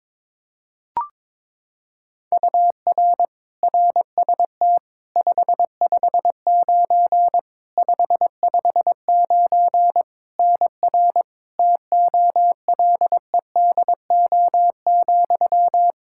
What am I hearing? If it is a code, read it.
Morse code, heard as a keyed tone, about 22 wpm, 700 Hz. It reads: UR RST 559 559 NR TOLEDO,
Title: Practice QSO - Exchange 1 - Beginning - 22wpm